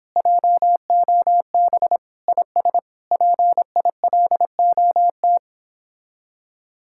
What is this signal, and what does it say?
Morse code, heard as a keyed tone, about 29 wpm, 700 Hz. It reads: JOB IS PILOT